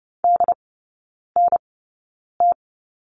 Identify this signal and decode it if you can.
Morse code, heard as a keyed tone, about 30 wpm, spaced as for 9 wpm, 700 Hz. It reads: DNT
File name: Lesson 16e - Letter c - Three Characters 30wpm